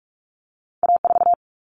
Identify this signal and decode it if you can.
Morse code, heard as a keyed tone, about 46 wpm, 700 Hz. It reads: A4
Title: Common Cut Numbers - 2-Digit 45wpm